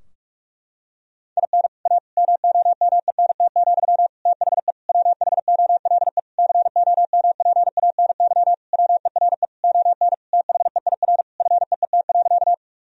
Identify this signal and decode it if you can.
Morse code, heard as a keyed tone, about 46 wpm, 700 Hz. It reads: IN A MOMENT, THE WHOLE COMPANY WERE ON THEIR FEET.
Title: Sentences from Top 1000 Words - Review Part 3 - 45wpm